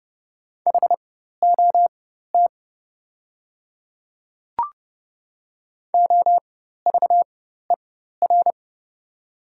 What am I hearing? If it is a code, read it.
Morse code, heard as a keyed tone, about 30 wpm, spaced as for 14 wpm, 700 Hz. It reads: HOT OVER
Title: 4x Character Spacing - Top 100 Words - 30wpm